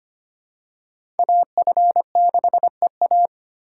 Morse code, heard as a keyed tone, about 25 wpm, 700 Hz. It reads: AF6EA
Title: USA Call Signs - 5 Characters - 2x Character Spacing - 25wpm